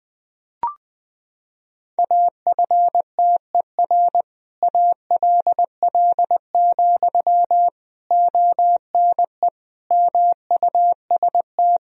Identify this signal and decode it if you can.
Morse code, heard as a keyed tone, about 20 wpm, 700 Hz. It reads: AFTER ALL, ONE MUST
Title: Sentences from Top 1000 Words - Review Part 3 - 20wpm